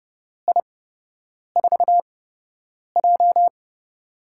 Morse code, heard as a keyed tone, about 30 wpm, spaced as for 8 wpm, 700 Hz. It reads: I4J
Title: USA Call Signs - 4 Characters - 8x Character Spacing - 30wpm